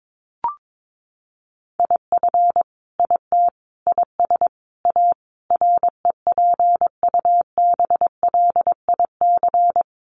Morse code, heard as a keyed tone, about 22 wpm, 700 Hz. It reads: IF IT IS A REPUBLIC
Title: Sentences from Top 1500 Words 22wpm